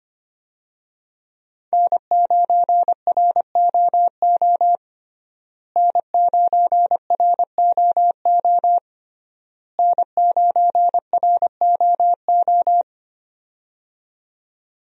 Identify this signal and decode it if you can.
Morse code, heard as a keyed tone, about 25 wpm, 700 Hz. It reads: N9ROO N9ROO N9ROO
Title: USA Call Signs - 5 Characters - Repeated 3x - 25wpm